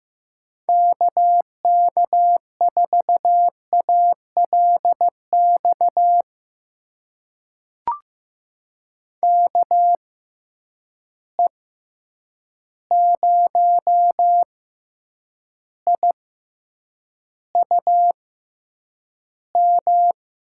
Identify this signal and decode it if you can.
Morse code, heard as a keyed tone, about 15 wpm, spaced as for 8 wpm, 700 Hz. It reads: KK4ALX KE0IUM
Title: USA Call Signs - 6 Characters - 6x Character Spacing - 15wpm